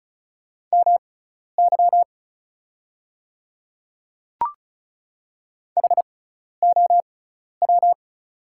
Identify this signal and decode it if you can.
Morse code, heard as a keyed tone, about 34 wpm, spaced as for 12 wpm, 690 Hz. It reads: MY HOW
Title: 6x Character Spacing - Top 200 Words - 35wpm